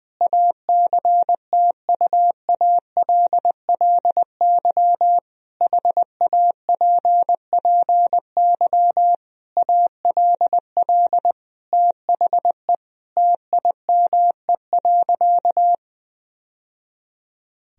Morse code, heard as a keyed tone, about 20 wpm, 700 Hz. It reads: ACTUALLY HAPPY ALL THE TIME.
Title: Sentences from Top 1000 Words - Review Part 4 - 20wpm